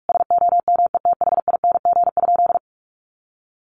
Morse code, heard as a keyed tone, about 44 wpm, 700 Hz. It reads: SOMETHING?
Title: Sentences from Top 1100 Words 45wpm